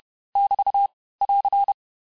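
Morse code, heard as a keyed tone, about 31 wpm, 800 Hz. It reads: <BT> <AR>
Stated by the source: Morse lesson of words, each word repeated 4 times, at 29 words per minute.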